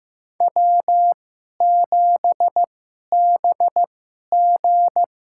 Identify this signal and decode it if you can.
Morse code, heard as a keyed tone, about 15 wpm, spaced as for 11 wpm, 700 Hz. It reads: W7BG
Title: USA Call Signs - 2x Character Spacing - 15wpm